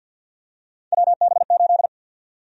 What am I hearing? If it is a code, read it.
Morse code, heard as a keyed tone, about 52 wpm, 700 Hz. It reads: WB8
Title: USA Call Sign Prefixes - 50wpm